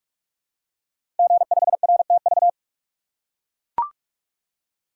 Morse code, heard as a keyed tone, about 44 wpm, 700 Hz. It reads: G5RTV
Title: Code Groups - 5 Letter:Numbers - 45wpm